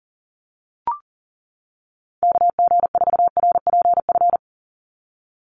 Morse code, heard as a keyed tone, about 40 wpm, 700 Hz. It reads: KG4RPF
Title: USA Call Signs 40wpm